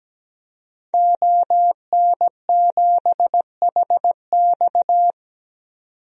Morse code, heard as a keyed tone, about 17 wpm, 700 Hz. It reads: ON7HX
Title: DXCC-100 Call Signs - 5 Characters - 17wpm